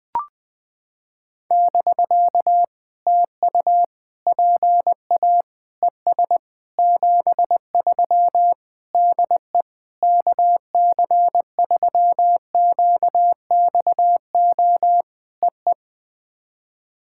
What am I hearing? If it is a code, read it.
Morse code, heard as a keyed tone, about 20 wpm, 700 Hz. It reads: <BK> TU PA ES 73 DE KC3QXO EE
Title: POTA QSO Practice - 20wpm